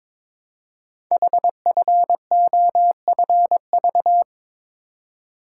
Morse code, heard as a keyed tone, about 22 wpm, 700 Hz. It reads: HFOFV
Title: Code Groups - 5 Letters - 22wpm